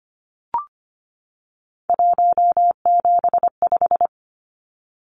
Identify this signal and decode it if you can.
Morse code, heard as a keyed tone, about 25 wpm, 700 Hz. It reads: 175
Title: Numbers - 3-Digit 25wpm